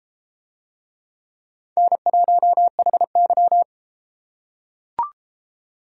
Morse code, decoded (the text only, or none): N1HY